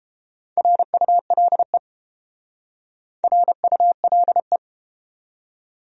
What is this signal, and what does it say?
Morse code, heard as a keyed tone, about 30 wpm, 700 Hz. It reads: RULE RULE